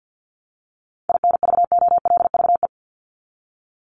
Morse code, heard as a keyed tone, about 50 wpm, 700 Hz. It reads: INVOLVE
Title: ING Suffix - Top 100 Words with Base Words 50wpm